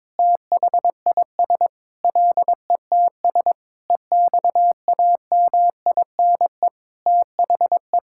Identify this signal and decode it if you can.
Morse code, heard as a keyed tone, about 22 wpm, 700 Hz. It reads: THIS LETS EXAMINE THE
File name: Sentences with 9-Word Phrases - 22wpm